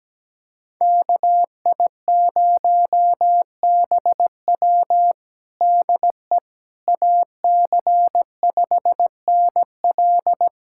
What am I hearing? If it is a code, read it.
Morse code, heard as a keyed tone, about 17 wpm, 700 Hz. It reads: KI0BW DE AC5NL
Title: USA Callsign DE Callsign - 5 Characters - 17wpm